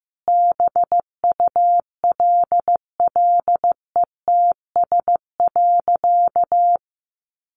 Morse code, heard as a keyed tone, about 14 wpm, 700 Hz. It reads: BULLETS.
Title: A Sound Of Thunder 15wpm